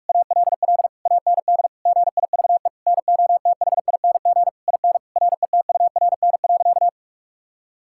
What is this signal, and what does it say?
Morse code, heard as a keyed tone, about 42 wpm, 690 Hz. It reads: ALL AND GIVE NOTHING IN RETURN.
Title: Sentences from Top 1000 Words - Review Part 6 - 45wpm